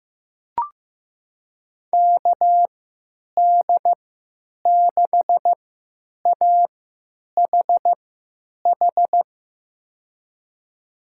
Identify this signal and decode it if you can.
Morse code, heard as a keyed tone, about 15 wpm, spaced as for 9 wpm, 700 Hz. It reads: KD6AHH
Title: USA Call Signs - 3x Character Spacing - 15wpm